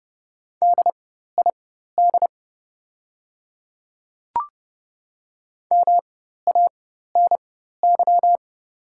Morse code, heard as a keyed tone, about 30 wpm, spaced as for 14 wpm, 700 Hz. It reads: DID MANY
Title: 4x Character Spacing - Top 200 Words - 30wpm